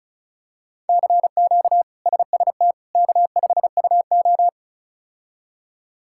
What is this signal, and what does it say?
Morse code, heard as a keyed tone, about 36 wpm, 700 Hz. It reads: CQ SST K5UO
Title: K1USN SST QSO Practice - 35wpm